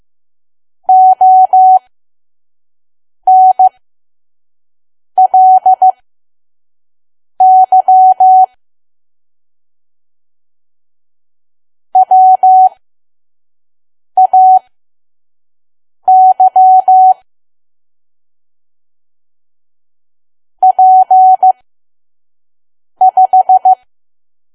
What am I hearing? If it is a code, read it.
Morse code, heard as a keyed tone, about 15 wpm, spaced as for 5 wpm, 750 Hz. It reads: ONLY WAY P5
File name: Archive / 5 WPM files / PM